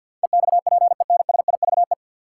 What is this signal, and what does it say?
Morse code, heard as a keyed tone, about 47 wpm, 700 Hz. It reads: EXPENSIVE
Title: Sentences from Top 1500 Words - New Word Drill Alt - 50wpm